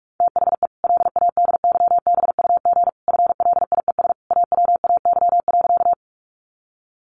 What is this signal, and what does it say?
Morse code, heard as a keyed tone, about 44 wpm, 700 Hz. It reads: THE LADYBUG FLIES AWAY.
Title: Additional Easy Phrases 45wpm